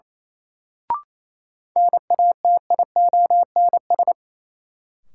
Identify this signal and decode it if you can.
Morse code, heard as a keyed tone, about 28 wpm, 700 Hz. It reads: NATIONS
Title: Sets of 3 Words - New Word Drill - 28wpm